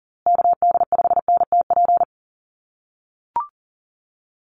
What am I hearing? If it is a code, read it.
Morse code, heard as a keyed tone, about 40 wpm, 700 Hz. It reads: KD5NTP
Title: USA Call Signs Repeated 2x 40wpm